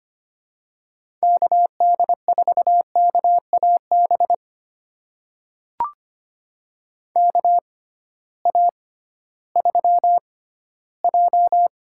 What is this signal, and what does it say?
Morse code, heard as a keyed tone, about 25 wpm, spaced as for 14 wpm, 700 Hz. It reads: KD4KAB KA3J
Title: USA Call Signs - 6 Characters - 6x Character Spacing - 25wpm